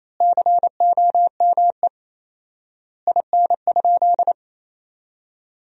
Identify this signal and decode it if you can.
Morse code, heard as a keyed tone, about 28 wpm, 700 Hz. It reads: COME IN?
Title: 4x Word-Spacing - Sentences from Top 100 Words - 28wpm